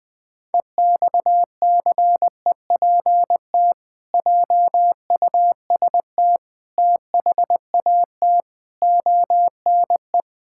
Morse code, heard as a keyed tone, about 20 wpm, 700 Hz. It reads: EXCEPT JUST THAT ONE
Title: Sentences from Top 1600 Words 20wpm